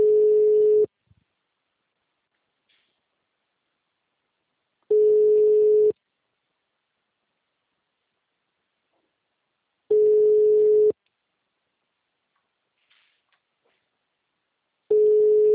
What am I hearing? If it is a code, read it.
Ringback tone (Europe, 425 Hz).